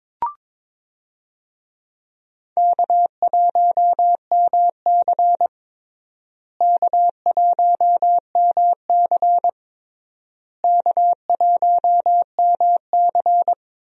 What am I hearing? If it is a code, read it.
Morse code, heard as a keyed tone, about 22 wpm, 700 Hz. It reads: K1MC K1MC K1MC
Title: USA Call Signs - 4 Characters - Repeated 3x - 22wpm